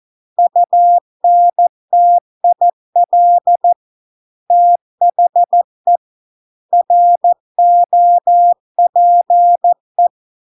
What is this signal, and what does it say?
Morse code, heard as a keyed tone, about 14 wpm, 700 Hz. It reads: UNTIL THE ROPE